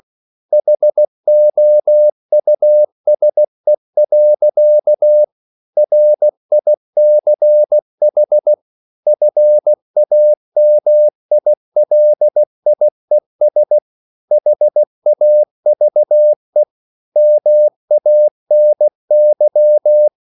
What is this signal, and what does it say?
Morse code, heard as a keyed tone, about 16 wpm, 600 Hz. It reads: HOUSE. RICH FAMILIES HAVE MANY